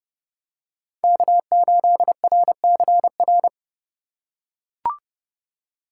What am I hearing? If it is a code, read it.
Morse code, heard as a keyed tone, about 30 wpm, 700 Hz. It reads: K8RCR